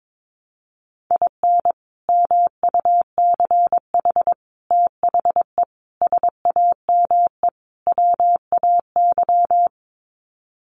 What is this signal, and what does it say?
Morse code, heard as a keyed tone, about 22 wpm, 700 Hz. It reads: IN MUCH THE SAME WAY
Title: Sets of 5 Words Encore 22wpm